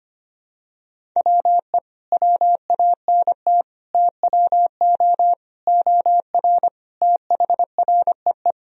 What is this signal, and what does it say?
Morse code, heard as a keyed tone, about 25 wpm, 700 Hz. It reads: WE WANT TWO OR THREE
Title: Sentences from Top 1000 Words - Review Part 1 - 25wpm